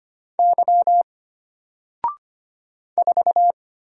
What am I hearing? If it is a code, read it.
Morse code, heard as a keyed tone, about 25 wpm, 700 Hz. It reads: Y 4
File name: Single-Letter-Number-25wpm